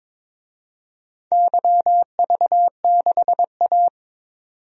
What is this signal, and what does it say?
Morse code, heard as a keyed tone, about 22 wpm, 700 Hz. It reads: YV6A
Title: DXCC-100 Call Signs - 4 Characters - 22wpm